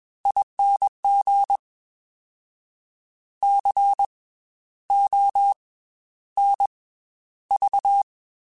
Morse code, heard as a keyed tone, about 21 wpm, 800 Hz. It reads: ING C O N V
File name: Special INGs / Special INGs-mp3s / Slow ing @20